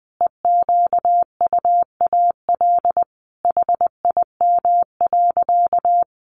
Morse code, heard as a keyed tone, about 20 wpm, 700 Hz. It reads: EQUAL HIM.
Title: Sentences from Top 1400 Words 20wpm